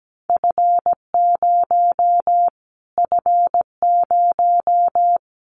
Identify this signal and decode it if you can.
Morse code, heard as a keyed tone, about 17 wpm, 700 Hz. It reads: F0 F0
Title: Callsign Prefixes for Top 25 DXCC - Repeated 3x - 17wpm